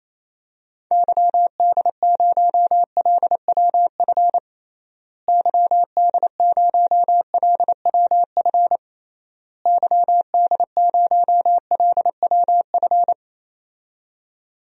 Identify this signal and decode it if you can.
Morse code, heard as a keyed tone, about 28 wpm, 700 Hz. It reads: YD0LWF YD0LWF YD0LWF
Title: DXCC-100 Call Signs - 6 Characters - Repeated 3x - 28wpm